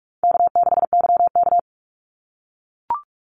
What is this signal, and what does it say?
Morse code, heard as a keyed tone, about 44 wpm, 700 Hz. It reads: K6YK
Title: NA SOTA Chaser Callsigns 45wpm